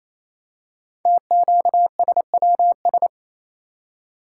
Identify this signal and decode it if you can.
Morse code, heard as a keyed tone, about 28 wpm, 700 Hz. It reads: TQSWS
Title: Code Groups - 5 Letters - 28wpm